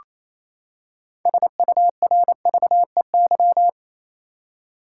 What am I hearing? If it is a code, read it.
Morse code, heard as a keyed tone, about 28 wpm, 700 Hz. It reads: SURVEY
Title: Six Letter Words - 28wpm